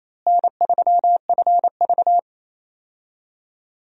Morse code, heard as a keyed tone, about 28 wpm, 700 Hz. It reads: N3FV
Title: USA Call Signs - 4 Characters - Repeated 3x - 28wpm